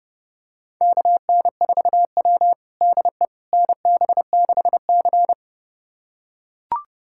Morse code, heard as a keyed tone, about 29 wpm, 700 Hz. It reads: KN4W DE NB6C
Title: USA Callsign DE Callsign - 4 Characters - 30wpm